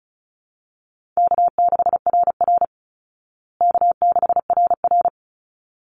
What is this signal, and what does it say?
Morse code, heard as a keyed tone, about 34 wpm, 700 Hz. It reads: K6RR K6RR
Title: Call Signs Repeated 3x - 35wpm